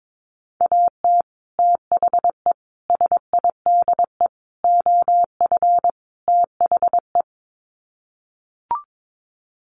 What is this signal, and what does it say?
Morse code, heard as a keyed tone, about 22 wpm, 700 Hz. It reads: AT THE SIDE OF THE
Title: Sets of 5 Words Encore2 22wpm